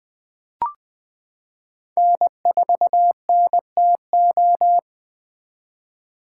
Morse code, heard as a keyed tone, about 20 wpm, 700 Hz. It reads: N4NTO
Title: 2023 POTA Activator Call Signs - 20wpm